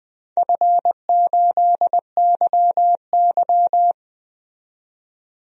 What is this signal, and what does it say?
Morse code, heard as a keyed tone, about 20 wpm, 700 Hz. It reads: F8YY
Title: DXCC-100 Call Signs - 4 Characters - 20wpm